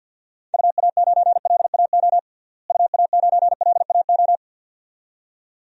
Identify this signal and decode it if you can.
Morse code, heard as a keyed tone, about 50 wpm, 700 Hz. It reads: UA9LAO UA9LAO
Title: DXCC-100 Call Signs - 6 Characters - Repeated 2x - 50wpm